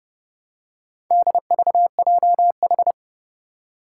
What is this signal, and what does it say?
Morse code, heard as a keyed tone, about 30 wpm, 700 Hz. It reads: DVJH
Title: Four Challenging Letter-Number Combinations 30wpm